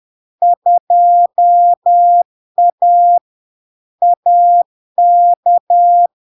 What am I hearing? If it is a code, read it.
Morse code, heard as a keyed tone, about 10 wpm, 700 Hz. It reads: 2A AK
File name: LARK-FD-2022-Log-in-CW-10wpm-700-Hz